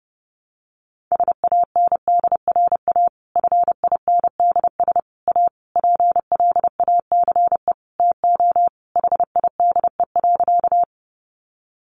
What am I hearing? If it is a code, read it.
Morse code, heard as a keyed tone, about 30 wpm, 700 Hz. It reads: SANDRA FINDS A PLACE TO HIDE.
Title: Easy Phrases 30wpm